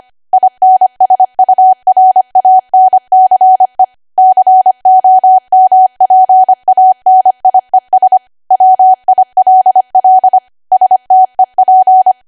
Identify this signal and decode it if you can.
Morse code, heard as a keyed tone, about 25 wpm, 750 Hz. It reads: INSURANCE COMPANIES WILL STEP
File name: cwa / W1AW-Level3 / 25WPM